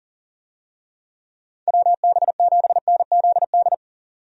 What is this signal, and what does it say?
Morse code, heard as a keyed tone, about 40 wpm, 700 Hz. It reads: WB7NZD
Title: USA Call Signs - 6 Characters - 40wpm